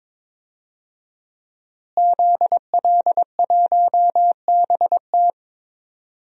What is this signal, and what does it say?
Morse code, heard as a keyed tone, about 22 wpm, 700 Hz. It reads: ZL1BT
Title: DXCC-100 Call Signs - 5 Characters - Repeated 2x - 22wpm